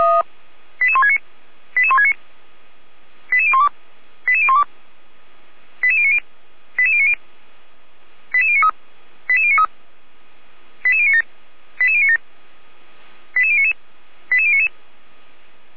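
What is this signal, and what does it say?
Five-tone selective call (ZVEI), heard as quick runs of five tones, tones of about 70 ms. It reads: 88169 88169 88012 88012 88090 88090 88093 88093 88097 88097 88099 88099
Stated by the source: ZVEI five-tone selective call received by SDR, on the feuerwehr-orlamuende.org fire brigade server.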